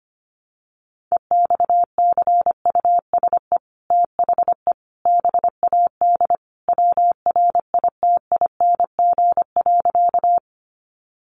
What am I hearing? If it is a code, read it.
Morse code, heard as a keyed tone, about 25 wpm, 700 Hz. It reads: EXCUSE THE BAD WRITING.
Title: Sentences from Top 800 Words 25wpm